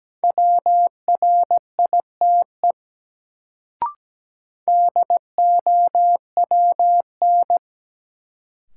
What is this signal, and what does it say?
Morse code, heard as a keyed tone, about 17 wpm, 700 Hz. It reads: WRITE DOWN
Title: Sentences from Top 300 Words - New Word Drill - 17wpm